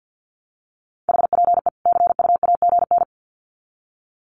Morse code, heard as a keyed tone, about 50 wpm, 700 Hz. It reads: HPE CU AGN
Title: Common QSO Phrases 50wpm